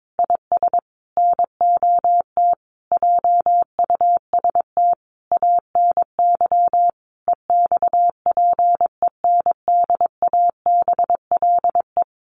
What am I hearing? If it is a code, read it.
Morse code, heard as a keyed tone, about 22 wpm, 700 Hz. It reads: IS NOT JUST ANY EXPENDABLE
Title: A Sound Of Thunder 22wpm